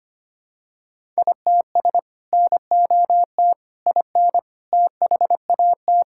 Morse code, heard as a keyed tone, about 25 wpm, 700 Hz. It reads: ITS NOT IN THAT
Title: Sentences from Top 1000 Words - Review Part 5 - 25wpm